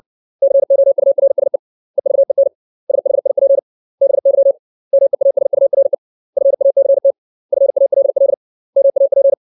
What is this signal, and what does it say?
Morse code, heard as a keyed tone, about 60 wpm, 550 Hz. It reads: COURSE EVEN SHIP BY MEASURE FACT FALL MAP
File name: 632 words / 632 Words 60WPM